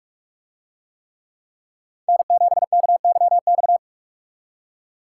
Morse code, heard as a keyed tone, about 44 wpm, 700 Hz. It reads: N7KYX